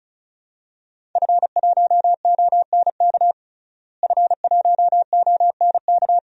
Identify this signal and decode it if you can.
Morse code, heard as a keyed tone, about 34 wpm, 700 Hz. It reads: F1ONK F1ONK
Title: DXCC-100 Call Signs - 5 Characters - Repeated 2x - 35wpm